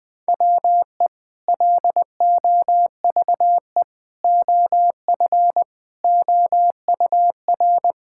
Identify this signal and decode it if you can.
Morse code, heard as a keyed tone, about 20 wpm, 700 Hz. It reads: WE LOVE OF OUR
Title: Sentences with 2-Word Phrases - 20wpm